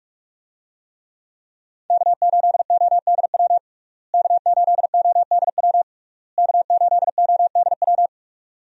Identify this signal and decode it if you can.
Morse code, heard as a keyed tone, about 43 wpm, 700 Hz. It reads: K8ODW K8ODW K8ODW
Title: USA Call Signs - 5 Characters - Repeated 3x - 45wpm